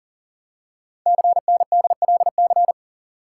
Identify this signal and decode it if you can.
Morse code, heard as a keyed tone, about 40 wpm, 700 Hz. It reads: CNDLC
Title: Code Groups - 5 Letters - 40wpm